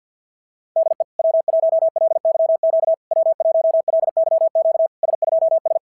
Morse code, heard as a keyed tone, about 50 wpm, 660 Hz. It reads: DE W1LYQ W1LYQ S2S
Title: SOTA QSO Practice 50wpm